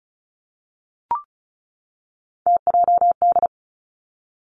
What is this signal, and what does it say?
Morse code, heard as a keyed tone, about 36 wpm, 700 Hz. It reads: TJD
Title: Three Challenging Letter-Number Combinations 35wpm